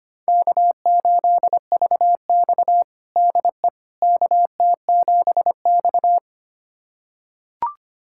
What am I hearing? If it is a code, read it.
Morse code, heard as a keyed tone, about 25 wpm, 700 Hz. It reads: K8VX DE KT7X
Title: USA Callsign DE Callsign - 4 Characters - 25wpm